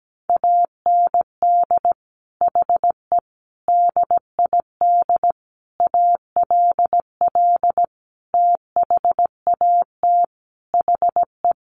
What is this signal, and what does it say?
Morse code, heard as a keyed tone, about 17 wpm, 700 Hz. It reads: AND HE DID ALL THAT HE
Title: Sentences from Top 1500 Words 17wpm